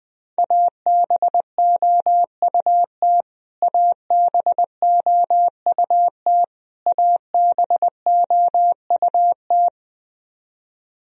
Morse code, heard as a keyed tone, about 20 wpm, 700 Hz. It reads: ABOUT ABOUT ABOUT
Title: Top 100 Words Repeated 20wpm